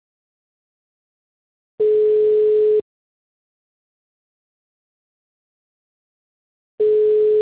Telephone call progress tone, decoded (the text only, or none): ringback tone